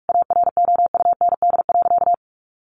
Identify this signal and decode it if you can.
Morse code, heard as a keyed tone, about 44 wpm, 700 Hz. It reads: AROUND.
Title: Easy Phrases 45wpm